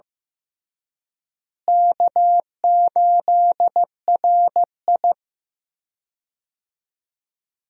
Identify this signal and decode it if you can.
Morse code, heard as a keyed tone, about 14 wpm, 700 Hz. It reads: K8RI
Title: USA Call Signs - 4 Characters - Repeated 3x - 15wpm